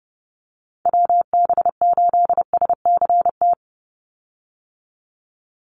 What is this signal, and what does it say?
Morse code, heard as a keyed tone, about 30 wpm, 700 Hz. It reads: WB8SCT
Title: Call Signs - 30wpm